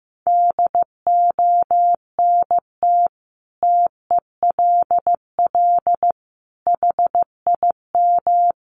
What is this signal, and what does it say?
Morse code, heard as a keyed tone, about 15 wpm, 700 Hz. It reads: DONT TELL HIM
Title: Sentences from Top 700 Words 15wpm